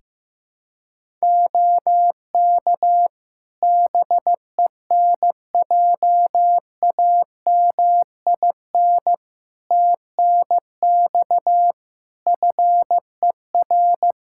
Morse code, heard as a keyed tone, about 15 wpm, 700 Hz. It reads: OK BENJAMIN TNX FER
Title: K1USN SST QSO Practice 15wpm